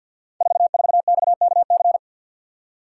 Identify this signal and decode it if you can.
Morse code, heard as a keyed tone, about 51 wpm, 700 Hz. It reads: 44XKC
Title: Code Groups - 5 Letter:Numbers - 50wpm